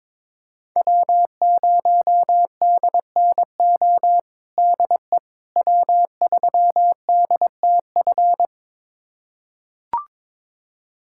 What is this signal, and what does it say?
Morse code, heard as a keyed tone, about 22 wpm, 700 Hz. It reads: W0DNO DE W3DTF
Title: USA Callsign DE Callsign - 5 Characters - 22wpm